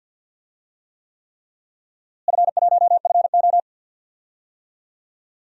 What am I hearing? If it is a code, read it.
Morse code, heard as a keyed tone, about 52 wpm, 700 Hz. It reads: F1FO
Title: DXCC-100 Call Signs - 4 Characters - Repeated 3x - 50wpm